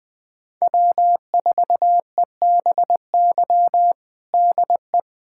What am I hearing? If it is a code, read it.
Morse code, heard as a keyed tone, about 20 wpm, 700 Hz. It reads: W4EBY DE
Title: USA Callsign DE Callsign - 5 Characters - 20wpm